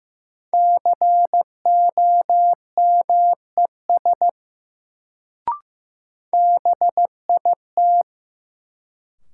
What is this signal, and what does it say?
Morse code, heard as a keyed tone, about 15 wpm, 700 Hz. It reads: COMES BIT
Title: Sets of 3 Words - New Word Drill - 15wpm